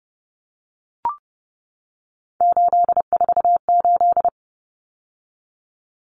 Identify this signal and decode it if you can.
Morse code, heard as a keyed tone, about 28 wpm, 700 Hz. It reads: 848
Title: Common Cut Numbers - 3-Digit 30wpm